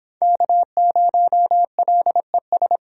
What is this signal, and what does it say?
Morse code, heard as a keyed tone, about 24 wpm, 700 Hz. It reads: K0LES